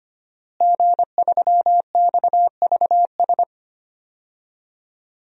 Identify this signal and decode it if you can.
Morse code, heard as a keyed tone, about 25 wpm, 700 Hz. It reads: G3XVS
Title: DXCC-100 Call Signs - 5 Characters - Repeated 2x - 25wpm